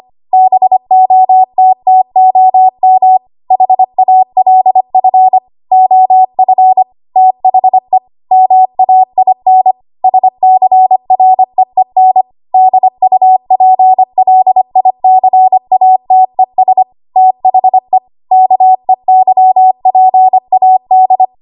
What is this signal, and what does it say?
Morse code, heard as a keyed tone, about 25 wpm, 750 Hz. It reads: BOTTOM HALF OF THE MAIN SCREEN DUPLICATES THE KEYPAD
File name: Archive / 25 WPM / 25WPM